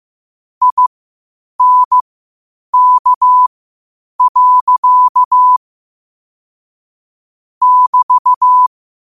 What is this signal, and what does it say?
Morse code, heard as a keyed tone, about 15 wpm, spaced as for 8 wpm, 1000 Hz. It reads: INK. <BT>
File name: qotdfiles / 8wpm